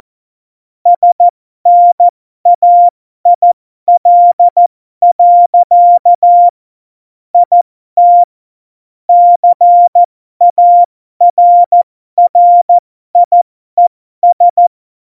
Morse code, heard as a keyed tone, about 14 wpm, spaced as for 12 wpm, 700 Hz. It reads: SNAIL. IT CARRIES
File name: Morsekurs / Witze